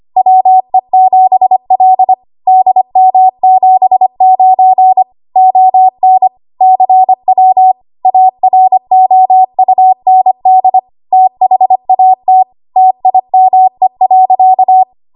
Morse code, heard as a keyed tone, about 25 wpm, 750 Hz. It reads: WE7L DM79 ON CW AROUND THAT TIME.
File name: Morse / 25WPM